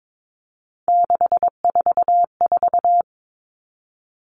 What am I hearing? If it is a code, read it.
Morse code, heard as a keyed tone, about 22 wpm, 700 Hz. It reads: 644
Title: All Cut Numbers - 3-Digit 22wpm